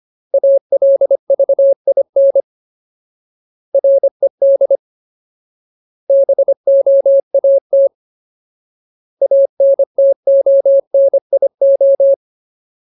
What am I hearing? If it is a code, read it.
Morse code, heard as a keyed tone, about 25 wpm, 550 Hz. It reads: ALVIN RED BOAT ANTONIO
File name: Ultimate Word List / 25WPM Ultimate